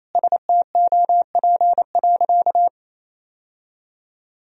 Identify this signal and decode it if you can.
Morse code, heard as a keyed tone, about 28 wpm, 700 Hz. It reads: STOP.
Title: Sentences from Top 1000 Words - Review Part 3 - 28wpm